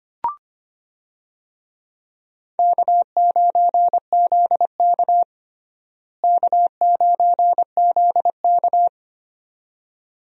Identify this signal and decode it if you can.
Morse code, heard as a keyed tone, about 25 wpm, 700 Hz. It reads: K9ZK K9ZK
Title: USA Call Signs - 4 Characters - Repeated 2x - 25wpm